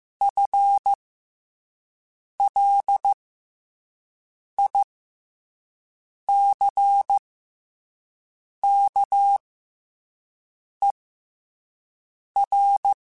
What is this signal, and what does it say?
Morse code, heard as a keyed tone, about 15 wpm, spaced as for 5 wpm, 800 Hz. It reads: FLICKER
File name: FL Runs / FL Runs mp3s / FL Runs @5